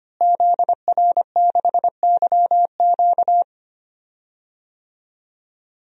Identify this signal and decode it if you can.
Morse code, heard as a keyed tone, about 25 wpm, 700 Hz. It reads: ZR6YQ
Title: DXCC-100 Call Signs - 5 Characters - Repeated 3x - 25wpm